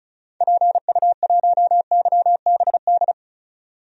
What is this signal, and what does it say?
Morse code, heard as a keyed tone, about 34 wpm, 700 Hz. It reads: PU1YBD
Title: DXCC-100 Call Signs - 6 Characters - 35wpm